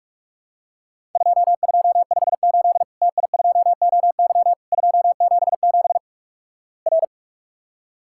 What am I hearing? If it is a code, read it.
Morse code, heard as a keyed tone, about 46 wpm, 700 Hz. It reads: 2258 TI2OY 277 R
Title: Worked All Europe DX Contest - QTC Exchange - 45wpm